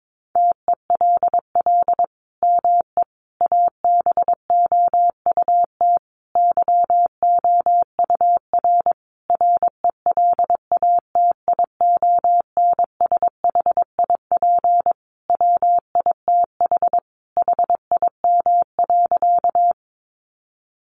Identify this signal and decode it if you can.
Morse code, heard as a keyed tone, about 22 wpm, 700 Hz. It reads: TELL ME ABOUT YOUR RELATIONSHIP WITH HIM.
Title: Sentences from Top 1100 Words 22wpm